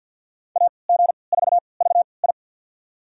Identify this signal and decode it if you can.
Morse code, heard as a keyed tone, about 49 wpm, spaced as for 28 wpm, 690 Hz. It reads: AG4VI
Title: USA Call Signs - 5 Characters - 3x Character Spacing - 50wpm